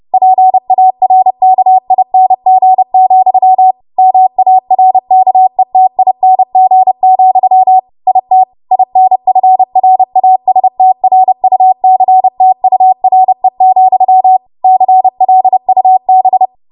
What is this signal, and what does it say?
Morse code, heard as a keyed tone, about 30 wpm, 750 Hz. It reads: PARKING, MARKETING, IT INFRASTRUCTURE, CLUB
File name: Morse / 30WPM